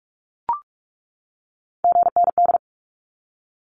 Morse code, heard as a keyed tone, about 44 wpm, 700 Hz. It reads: GND
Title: Abbreviations 45wpm